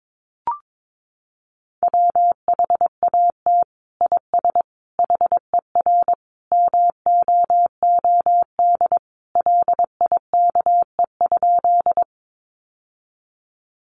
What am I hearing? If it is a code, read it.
Morse code, heard as a keyed tone, about 22 wpm, 700 Hz. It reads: WHAT IS HER MOOD LIKE?
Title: Sentences from Top 1100 Words 22wpm